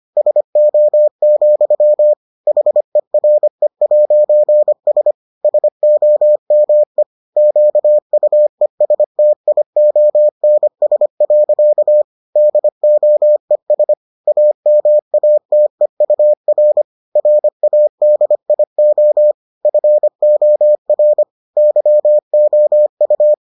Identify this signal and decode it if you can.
Morse code, heard as a keyed tone, about 25 wpm, 600 Hz. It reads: SO, HERE'S SOME QUESTIONS. DOES AMATEUR RADIO FOR YOU